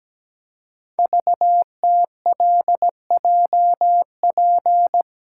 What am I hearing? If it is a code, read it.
Morse code, heard as a keyed tone, about 17 wpm, 700 Hz. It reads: VTLJP
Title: Code Groups - 5 Letters - 17wpm